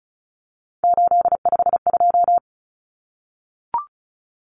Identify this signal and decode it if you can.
Morse code, heard as a keyed tone, about 34 wpm, 700 Hz. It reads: 852
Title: Common Cut Numbers - 3-Digit 35wpm